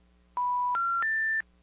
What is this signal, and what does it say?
Special information tone.
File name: 575-385-9999 valor